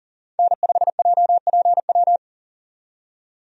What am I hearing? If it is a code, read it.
Morse code, heard as a keyed tone, about 40 wpm, 700 Hz. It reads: N5JPW
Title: USA Call Signs - 5 Characters - 40wpm